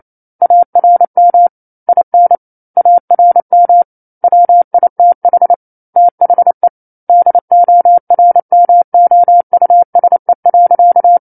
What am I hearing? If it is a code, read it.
Morse code, heard as a keyed tone, about 28 wpm, 700 Hz. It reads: ARM IN ARM WITH THE DORMOUSE.
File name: 09-Chapter-09 28 WPM